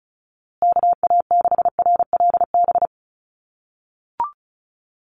Morse code, heard as a keyed tone, about 34 wpm, 700 Hz. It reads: KA6RLB
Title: USA Call Signs Repeated 3x 35wpm